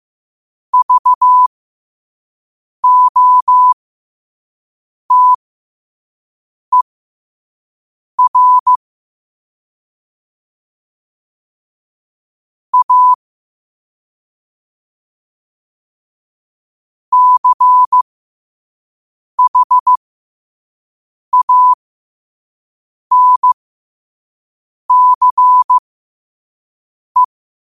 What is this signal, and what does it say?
Morse code, heard as a keyed tone, about 15 wpm, spaced as for 5 wpm, 1000 Hz. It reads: VOTER A CHANCE
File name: qotdfiles / 5wpm